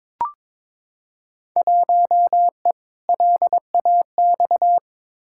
Morse code, heard as a keyed tone, about 22 wpm, 700 Hz. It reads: 1E LAX
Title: Field Day Exchanges - 22wpm